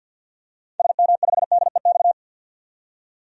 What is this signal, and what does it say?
Morse code, heard as a keyed tone, about 49 wpm, 700 Hz. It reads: SM5DEX
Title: DXCC-100 Call Signs - 6 Characters - 50wpm